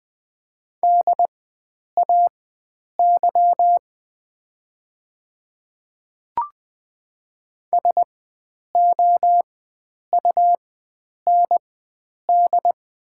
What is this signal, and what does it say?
Morse code, heard as a keyed tone, about 20 wpm, spaced as for 9 wpm, 700 Hz. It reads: DAY SOUND
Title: 4x Character Spacing - Top 100 Words - 20wpm